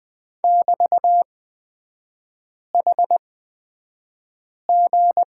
Morse code, heard as a keyed tone, about 20 wpm, spaced as for 5 wpm, 700 Hz. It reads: <BT>HG
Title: 224 Lesson 36e - BT - Three Characters 20wpm